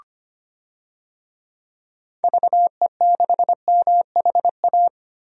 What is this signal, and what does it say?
Morse code, heard as a keyed tone, about 25 wpm, 700 Hz. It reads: VE6MHA
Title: DXCC-100 Call Signs - 6 Characters - 25wpm